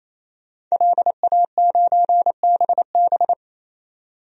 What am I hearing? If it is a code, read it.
Morse code, heard as a keyed tone, about 28 wpm, 700 Hz. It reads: LA9BB
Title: DXCC-100 Call Signs - 5 Characters - 28wpm